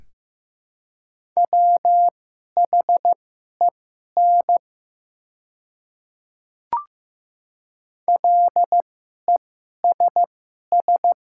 Morse code, heard as a keyed tone, about 15 wpm, spaced as for 11 wpm, 700 Hz. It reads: WHEN LESS
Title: 2x Character Spacing - Top 500 Words - 15wpm